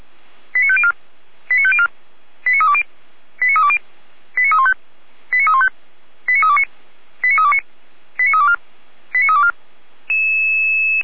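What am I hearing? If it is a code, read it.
Five-tone selective call (ZVEI), heard as quick runs of five tones, tones of about 70 ms. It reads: 89504 89504 89320 89320 89315 89315 89329 89329 89324 89324
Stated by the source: ZVEI five-tone call 89512 received by SDR.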